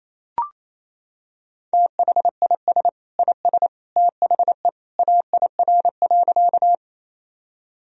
Morse code, heard as a keyed tone, about 27 wpm, 700 Hz. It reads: THIS IS THE AIR.
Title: Sentences from Top 1000 Words - Review Part 1 - 28wpm